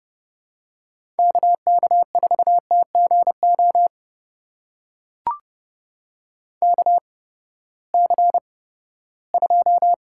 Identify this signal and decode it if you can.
Morse code, heard as a keyed tone, about 30 wpm, spaced as for 17 wpm, 700 Hz. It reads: KK4TGO KC2
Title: USA Call Signs - 6 Characters - 8x Character Spacing - 30wpm